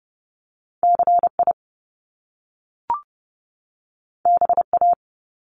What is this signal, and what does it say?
Morse code, heard as a keyed tone, about 30 wpm, 700 Hz. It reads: CI BA
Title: Top Two Letter Combinations in Words 30wpm